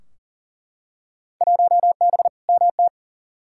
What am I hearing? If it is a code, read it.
Morse code, heard as a keyed tone, about 40 wpm, 700 Hz. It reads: 1B MT
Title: Field Day Exchanges - 40wpm